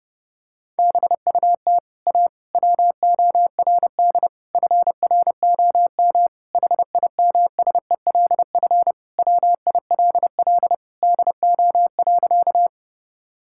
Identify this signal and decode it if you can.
Morse code, heard as a keyed tone, about 32 wpm, 700 Hz. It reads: BUT A WORD FROM HIMSELF WILL DO.
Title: Sentences from Top 1600 Words 30wpm